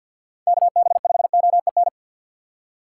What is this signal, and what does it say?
Morse code, heard as a keyed tone, about 52 wpm, 700 Hz. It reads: KB5OEN
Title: USA Call Signs - 6 Characters - 3x Character Spacing - 50wpm